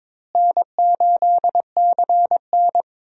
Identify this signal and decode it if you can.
Morse code, heard as a keyed tone, about 22 wpm, 700 Hz. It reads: N8CN
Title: USA Call Signs - 4 Characters - Repeated 2x - 22wpm